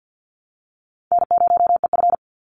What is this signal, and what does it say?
Morse code, heard as a keyed tone, about 50 wpm, 700 Hz. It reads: N0EF
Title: Call Signs - 50wpm